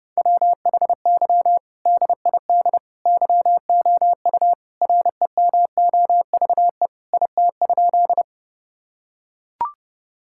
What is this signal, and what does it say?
Morse code, heard as a keyed tone, about 30 wpm, 700 Hz. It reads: WHY DID YOU REMOVE IT?
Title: Sentences from Top 1000 Words - Review Part 5 - 30wpm